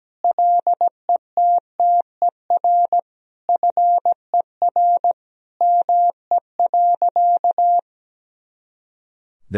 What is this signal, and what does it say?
Morse code, heard as a keyed tone, about 17 wpm, 700 Hz. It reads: LETTER FER ME.
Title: Sentences from Top 300 Words and Abbreviations 17wpm